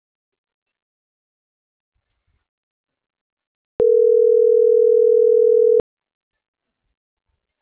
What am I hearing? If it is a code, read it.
Ringback tone (North America), 440 + 480 Hz.